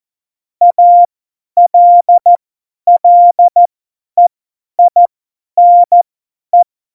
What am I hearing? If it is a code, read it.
Morse code, heard as a keyed tone, about 14 wpm, spaced as for 10 wpm, 700 Hz. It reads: ALLEINE
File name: Morsekurs / Witze / Witz 07. Januar 2020, 50 F70